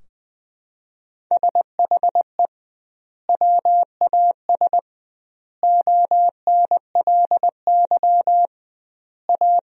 Morse code, heard as a keyed tone, about 20 wpm, 700 Hz. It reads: SHE WAS ONLY A
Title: 2x Word-Spacing - Sentences from Top 400 Words - 20wpm